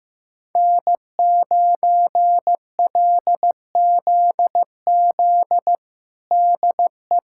Morse code, heard as a keyed tone, about 15 wpm, 700 Hz. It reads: N9LZZ DE
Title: USA Callsign DE Callsign - 5 Characters - 15wpm